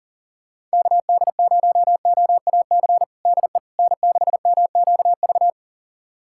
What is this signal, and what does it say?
Morse code, heard as a keyed tone, about 40 wpm, 700 Hz. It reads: KD0OAC DE N6MQV
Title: USA Callsign DE Callsign 40wpm